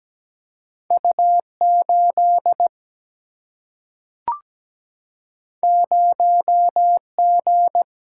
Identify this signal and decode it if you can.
Morse code, heard as a keyed tone, about 17 wpm, 700 Hz. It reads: U8 0G
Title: Single-Letter Single-Number 17wpm